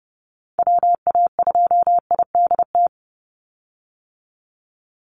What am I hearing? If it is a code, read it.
Morse code, heard as a keyed tone, about 30 wpm, 700 Hz. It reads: WA2IDT
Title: USA Call Signs 30wpm